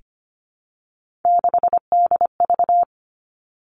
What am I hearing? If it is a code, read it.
Morse code, heard as a keyed tone, about 25 wpm, 700 Hz. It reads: 6DV
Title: All Cut Numbers - 3-Digit 25wpm